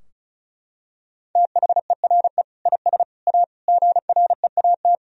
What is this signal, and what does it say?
Morse code, heard as a keyed tone, about 36 wpm, 700 Hz. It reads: THERE IS A GREAT
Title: Sentences from Top 1000 Words - Review Part 4 - 35wpm